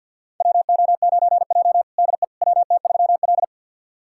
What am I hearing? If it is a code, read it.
Morse code, heard as a keyed tone, about 47 wpm, 700 Hz. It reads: WO9J DE WT3L
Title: USA Callsign DE Callsign - 4 Characters - 50wpm